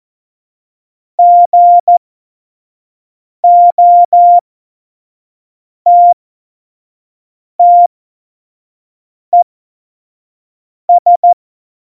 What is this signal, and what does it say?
Morse code, heard as a keyed tone, about 14 wpm, spaced as for 5 wpm, 700 Hz. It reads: GOTTES
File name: Morsekurs / Witze / Witz 22. Dezember 2020, 25 F70